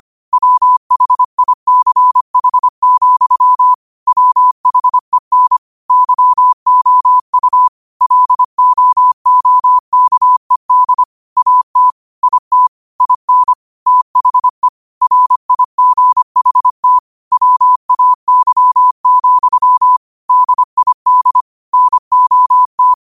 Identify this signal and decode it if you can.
Morse code, heard as a keyed tone, about 25 wpm, 1000 Hz. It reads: WHICH, WHEN YOU LOOKED AT IT IN THE RIGHT WAY, DID NOT